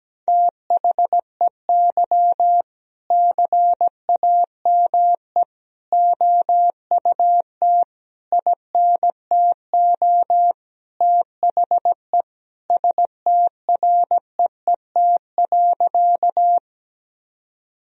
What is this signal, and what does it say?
Morse code, heard as a keyed tone, about 17 wpm, 700 Hz. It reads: THEY CAME OUT INTO THE STREET.
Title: Sentences from Top 1000 Words - Review Part 3 - 17wpm